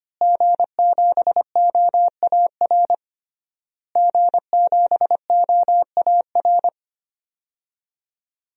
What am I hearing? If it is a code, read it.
Morse code, heard as a keyed tone, about 25 wpm, 700 Hz. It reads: G7OAR G7OAR
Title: DXCC-100 Call Signs - 5 Characters - Repeated 3x - 25wpm